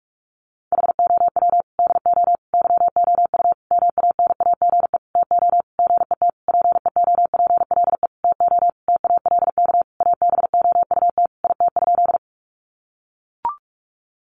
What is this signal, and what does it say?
Morse code, heard as a keyed tone, about 44 wpm, 700 Hz. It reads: HOW DO YOU MANAGE TO GET PEOPLE TO TALK ABOUT IT?
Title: Sentences from Top 1400 Words 45wpm